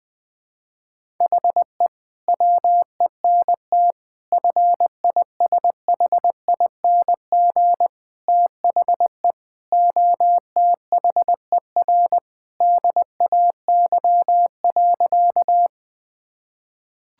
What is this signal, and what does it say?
Morse code, heard as a keyed tone, about 20 wpm, 700 Hz. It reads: HE WENT FISHING THE OTHER DAY.
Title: Sentences from Top 1000 Words - Review Part 5 - 20wpm